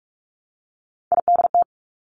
Morse code, heard as a keyed tone, about 46 wpm, 700 Hz. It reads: IDT
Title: Three Challenging Letter-Number Combinations 45wpm